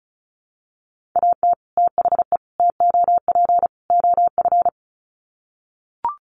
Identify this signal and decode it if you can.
Morse code, heard as a keyed tone, about 34 wpm, 700 Hz. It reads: AT THE TOP OF
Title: Sets of 4 Words 35wpm